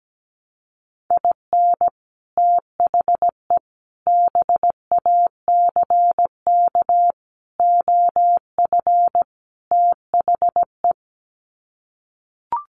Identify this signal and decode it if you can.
Morse code, heard as a keyed tone, about 17 wpm, 700 Hz. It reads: IN THE BACK OF THE
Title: Sets of 5 Words 17wpm